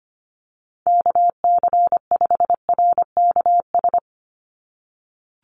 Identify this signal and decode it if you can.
Morse code, heard as a keyed tone, about 25 wpm, 700 Hz. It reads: KC5RKS
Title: USA Call Signs 25wpm